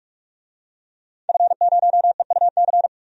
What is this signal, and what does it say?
Morse code, heard as a keyed tone, about 46 wpm, 700 Hz. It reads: F0EUC
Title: DXCC-100 Call Signs - 5 Characters - 45wpm